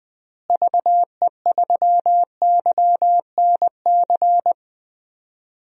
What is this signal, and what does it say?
Morse code, heard as a keyed tone, about 21 wpm, 690 Hz. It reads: VE3YNC
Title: DXCC-100 Call Signs - 6 Characters - Repeated 2x - 20wpm